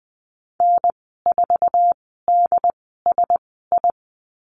Morse code, heard as a keyed tone, about 20 wpm, spaced as for 15 wpm, 700 Hz. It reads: N4DSI
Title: 069 Lesson 12g - Letter d - Call Signs 20wpm